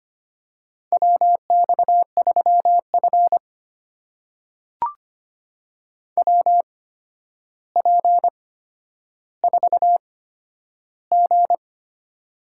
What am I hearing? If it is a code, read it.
Morse code, heard as a keyed tone, about 25 wpm, spaced as for 11 wpm, 700 Hz. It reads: WX3F WP4G
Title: USA Call Signs - 4 Characters - 8x Character Spacing - 25wpm